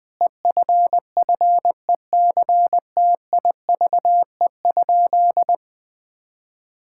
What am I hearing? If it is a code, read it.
Morse code, heard as a keyed tone, about 20 wpm, 700 Hz. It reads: EFFECTIVE?
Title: Sentences from Top 1000 Words - Review Part 6 - 20wpm